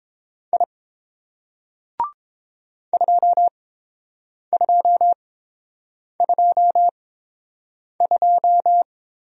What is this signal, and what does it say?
Morse code, heard as a keyed tone, about 30 wpm, spaced as for 7 wpm, 700 Hz. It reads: I 2222